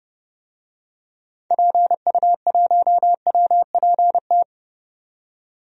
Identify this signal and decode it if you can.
Morse code, heard as a keyed tone, about 32 wpm, 700 Hz. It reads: PU1WPT
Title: DXCC-100 Call Signs - 6 Characters - 30wpm